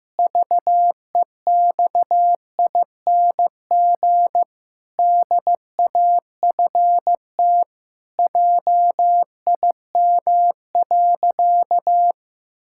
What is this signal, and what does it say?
Morse code, heard as a keyed tone, about 15 wpm, 700 Hz. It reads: VEXING DAFT JIM.
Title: Pangram Challenge v2 - 15wpm